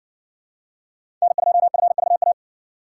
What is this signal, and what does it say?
Morse code, heard as a keyed tone, about 60 wpm, 700 Hz. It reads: N2FVA